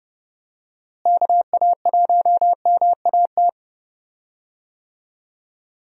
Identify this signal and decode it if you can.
Morse code, heard as a keyed tone, about 30 wpm, 700 Hz. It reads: KA1MAT
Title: USA Call Signs - 6 Characters - Repeated 3x - 30wpm